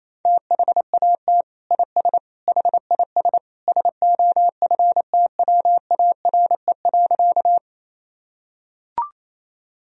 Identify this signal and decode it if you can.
Morse code, heard as a keyed tone, about 28 wpm, 700 Hz. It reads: THAT IS HIS SOFTWARE.
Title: Sentences from Top 1000 Words - Review Part 3 - 28wpm